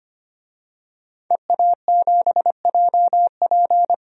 Morse code, heard as a keyed tone, about 25 wpm, 700 Hz. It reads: EA7JP
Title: DXCC-100 Call Signs - 5 Characters - 25wpm